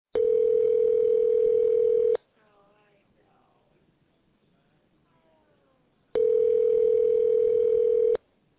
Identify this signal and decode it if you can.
Ringback tone (North America), 440 + 480 Hz.